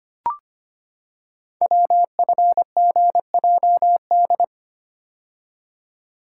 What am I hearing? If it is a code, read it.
Morse code, heard as a keyed tone, about 25 wpm, 700 Hz. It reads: WFGJD
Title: Code Groups - 5 Letter:Numbers - 25wpm